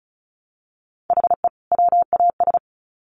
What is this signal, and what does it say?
Morse code, heard as a keyed tone, about 34 wpm, 700 Hz. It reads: HE WAS
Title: Sets of 2 Words 35wpm